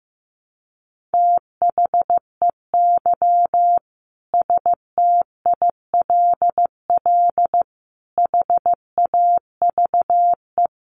Morse code, heard as a keyed tone, about 14 wpm, 700 Hz. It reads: THEY STILL HAVE
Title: Sentences from Top 1000 Words 15wpm